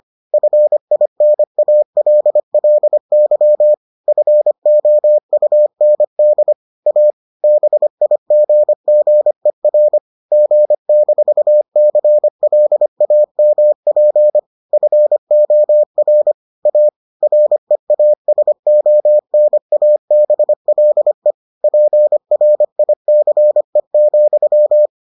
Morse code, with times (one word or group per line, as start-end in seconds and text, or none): 0.3-3.7 FINALLY
4.1-6.5 FOUND
6.9-7.1 A
7.4-10.0 BIGGER
10.3-14.4 G-CLAMP
14.7-16.3 FOR
16.7-16.9 A
17.2-21.3 REASONABLE
21.6-24.9 PRICE,